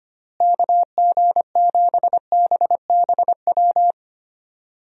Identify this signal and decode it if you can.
Morse code, heard as a keyed tone, about 25 wpm, 700 Hz. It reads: KG7BBW
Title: USA Call Signs - 6 Characters - Repeated 3x - 25wpm